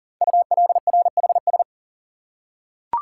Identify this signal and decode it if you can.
Morse code, heard as a keyed tone, about 40 wpm, 700 Hz. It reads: ULRHS
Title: Code Groups - 5 Letters - 40wpm